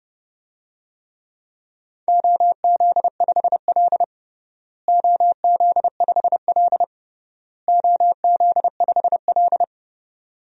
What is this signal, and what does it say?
Morse code, heard as a keyed tone, about 30 wpm, 700 Hz. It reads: OZ5L OZ5L OZ5L
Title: DXCC-100 Call Signs - 4 Characters - Repeated 3x - 30wpm